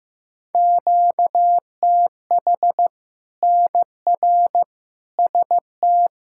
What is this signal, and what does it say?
Morse code, heard as a keyed tone, about 16 wpm, 700 Hz. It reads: QTH NR ST
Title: K1USN SST QSO Practice 15wpm